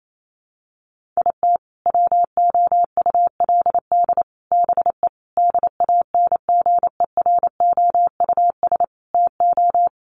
Morse code, heard as a keyed tone, about 28 wpm, 700 Hz. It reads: IT WOULD BE DANGEROUS TO
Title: Sentences from Top 1200 Words 28wpm